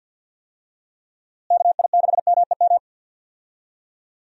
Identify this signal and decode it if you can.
Morse code, heard as a keyed tone, about 46 wpm, 700 Hz. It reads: KI6MEM
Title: USA Call Signs - 6 Characters - Repeated 3x - 50wpm